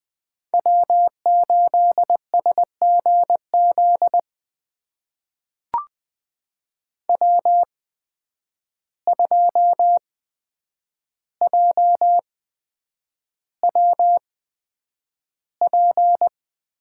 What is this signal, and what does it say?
Morse code, heard as a keyed tone, about 20 wpm, spaced as for 9 wpm, 700 Hz. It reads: W8SGZ W2JWP
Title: USA Call Signs - 5 Characters - 8x Character Spacing - 20wpm